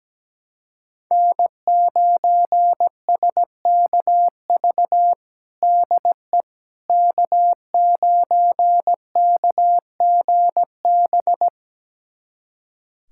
Morse code, heard as a keyed tone, about 17 wpm, 700 Hz. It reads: N9SKV DE K9KGB
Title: USA Callsign DE Callsign - 5 Characters - 17wpm